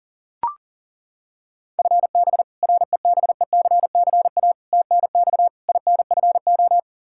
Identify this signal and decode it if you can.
Morse code, heard as a keyed tone, about 40 wpm, 700 Hz. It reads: FB REBECCA TNX INFO
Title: Practice QSO - Exchange 3 - More Info - 40wpm